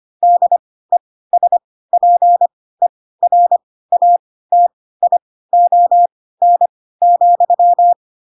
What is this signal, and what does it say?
Morse code, heard as a keyed tone, about 25 wpm, spaced as for 16 wpm, 700 Hz. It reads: DESPERATION,
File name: Pinocchio / Pinocchio ch6 10 25 ik1hge